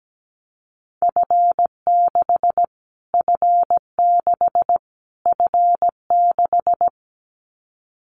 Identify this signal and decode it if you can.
Morse code, heard as a keyed tone, about 17 wpm, 700 Hz. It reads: F6 F6 F6
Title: Callsign Prefixes for Top 25 DXCC - Repeated 3x - 17wpm